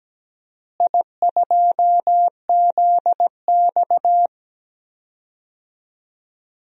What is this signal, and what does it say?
Morse code, heard as a keyed tone, about 17 wpm, 700 Hz. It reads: I2ZX
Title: DXCC-100 Call Signs - 4 Characters - Repeated 2x - 17wpm